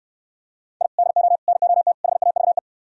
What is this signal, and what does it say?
Morse code, heard as a keyed tone, about 69 wpm, 700 Hz. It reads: I DO NOT HAVE